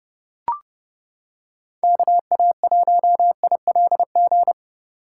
Morse code, heard as a keyed tone, about 30 wpm, 700 Hz. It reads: KA1ILG